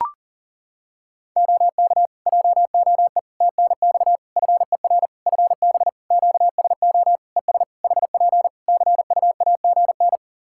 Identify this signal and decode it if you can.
Morse code, heard as a keyed tone, about 40 wpm, 700 Hz. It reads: OK JOE TNX FER FB QSO ES HP CUAGN
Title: Practice QSO - Exchange 4 - Ending - 40wpm